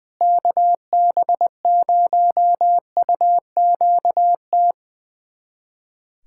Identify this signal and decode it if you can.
Morse code, heard as a keyed tone, about 20 wpm, 700 Hz. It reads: KB0UQT
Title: USA Callsign DE Callsign 20wpm